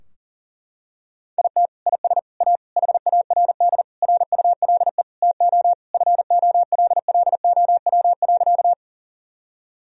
Morse code, heard as a keyed tone, about 37 wpm, 700 Hz. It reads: IT IS A HARD RULE TO FOLLOW.
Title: Sentences from Top 1000 Words - Review Part 2 - 40wpm